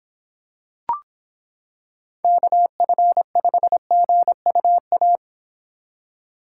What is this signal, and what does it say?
Morse code, heard as a keyed tone, about 26 wpm, 700 Hz. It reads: KF5GUA